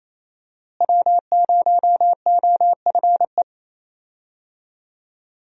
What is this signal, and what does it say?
Morse code, heard as a keyed tone, about 27 wpm, 700 Hz. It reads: W0OFE
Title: USA Call Signs - 5 Characters - Repeated 3x - 28wpm